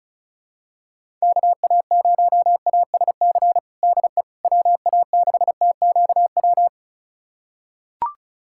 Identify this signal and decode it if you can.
Morse code, heard as a keyed tone, about 34 wpm, 700 Hz. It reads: KA0ASC DE WA6TQW
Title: USA Callsign DE Callsign - 6 Characters - 35wpm